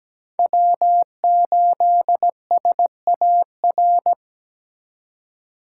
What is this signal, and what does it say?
Morse code, heard as a keyed tone, about 20 wpm, 700 Hz. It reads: W8SAR
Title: Speed Racing - USA 1x3 Call Signs 17wpm